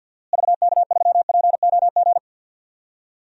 Morse code, heard as a keyed tone, about 48 wpm, 700 Hz. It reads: VK3POG
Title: DXCC-100 Call Signs - 6 Characters - Repeated 2x - 50wpm